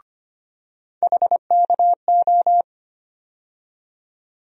Morse code, heard as a keyed tone, about 25 wpm, 700 Hz. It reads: HKO
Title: USA Call Sign Postfixes - 3 Letters 25wpm